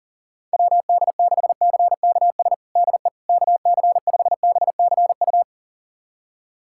Morse code, heard as a keyed tone, about 40 wpm, 700 Hz. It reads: WD6CKS DE KC5BCU
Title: USA Callsign DE Callsign - 6 Characters - 40wpm